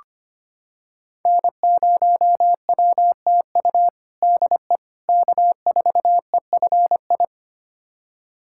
Morse code, heard as a keyed tone, about 25 wpm, 700 Hz. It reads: N0WTU DE K4EFI
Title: USA Callsign DE Callsign - 5 Characters - 25wpm